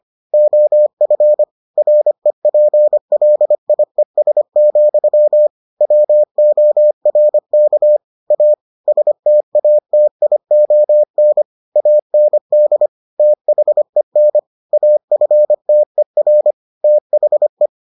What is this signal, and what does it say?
Morse code, heard as a keyed tone, about 25 wpm, 600 Hz. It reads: OF REPLIES, WORK A STATION AND THEN AFTER THE